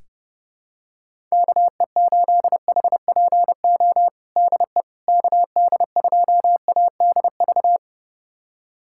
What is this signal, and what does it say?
Morse code, heard as a keyed tone, about 32 wpm, 700 Hz. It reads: KE8HPO DE KD2ADV
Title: USA Callsign DE Callsign - 6 Characters - 30wpm